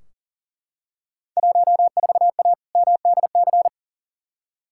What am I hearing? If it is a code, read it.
Morse code, heard as a keyed tone, about 37 wpm, 700 Hz. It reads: 14A MDC
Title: Field Day Exchanges Encore - 40wpm